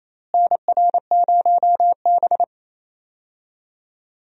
Morse code, heard as a keyed tone, about 28 wpm, 700 Hz. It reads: NR0B
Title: USA Call Signs - 4 Characters - Repeated 3x - 28wpm